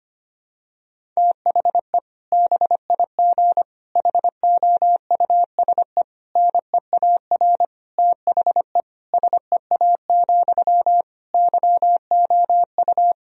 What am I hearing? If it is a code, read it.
Morse code, heard as a keyed tone, about 25 wpm, 700 Hz. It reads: THE BIG HOUSE NEAR THE SEA, YOU
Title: Sentences from Top 1000 Words - Review Part 1 - 25wpm